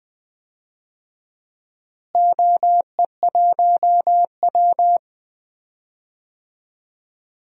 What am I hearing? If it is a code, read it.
Morse code, heard as a keyed tone, about 20 wpm, 700 Hz. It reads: OE1W
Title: DXCC-100 Call Signs - 4 Characters - Repeated 2x - 20wpm